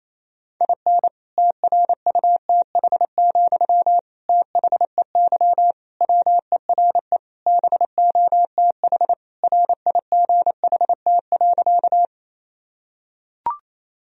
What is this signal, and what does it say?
Morse code, heard as a keyed tone, about 28 wpm, 700 Hz. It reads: IN TRUTH, THEY WERE BOTH RIGHT.
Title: Sentences from Top 1000 Words - Review Part 1 - 28wpm